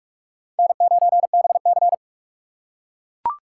Morse code, heard as a keyed tone, about 41 wpm, 700 Hz. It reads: N9BC